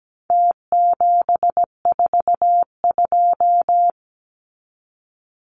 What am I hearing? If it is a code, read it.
Morse code, heard as a keyed tone, about 17 wpm, 700 Hz. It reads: T742
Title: Numbers - Contest Style - 4-Digit 17wpm